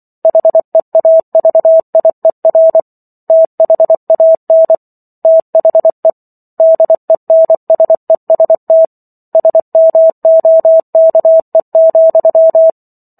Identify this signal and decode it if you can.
Morse code, heard as a keyed tone, about 24 wpm, 650 Hz. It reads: HEAVIER THAN THE DENSEST SMOKE,